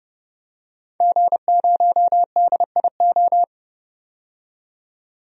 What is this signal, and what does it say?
Morse code, heard as a keyed tone, about 32 wpm, 700 Hz. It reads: G0DIO
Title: DXCC-100 Call Signs - 5 Characters - 30wpm